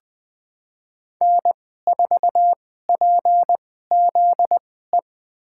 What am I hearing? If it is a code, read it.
Morse code, heard as a keyed tone, about 20 wpm, spaced as for 15 wpm, 700 Hz. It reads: N4PZE
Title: USA Call Signs - 5 Characters - 2x Character Spacing - 20wpm